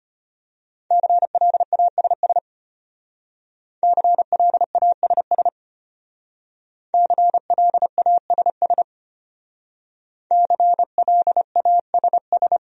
Morse code, heard as a keyed tone, about 30 wpm, 700 Hz. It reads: CLASS CLASS CLASS CLASS
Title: Speed-Racing Top 500 Words 25wpm